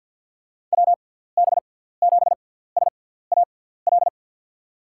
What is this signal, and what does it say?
Morse code, heard as a keyed tone, about 49 wpm, 700 Hz. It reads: W B 7 S A L
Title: USA Call Signs - 6 Characters - 6x Character Spacing - 50wpm